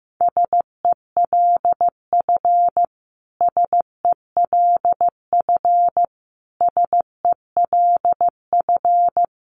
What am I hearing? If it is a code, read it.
Morse code, heard as a keyed tone, about 15 wpm, 700 Hz. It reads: SELF SELF SELF
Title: Top 500 Words Repeated 15wpm